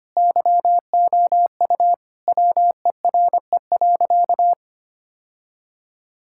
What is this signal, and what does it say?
Morse code, heard as a keyed tone, about 25 wpm, 700 Hz. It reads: YOU WERE.
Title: Sentences from Top 1000 Words - Review Part 6 - 25wpm